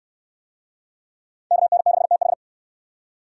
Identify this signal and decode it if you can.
Morse code, heard as a keyed tone, about 69 wpm, 700 Hz. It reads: KN6TH